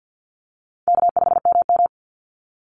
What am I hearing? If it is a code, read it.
Morse code, heard as a keyed tone, about 48 wpm, 700 Hz. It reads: K5MM
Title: Call Signs Repeated 2x - 50wpm